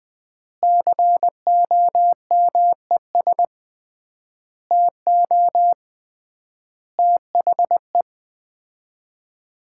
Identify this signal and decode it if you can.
Morse code, heard as a keyed tone, about 20 wpm, 700 Hz. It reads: COMES TO THE
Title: 3x Word-Spacing - Sets of 5 Words - 20wpm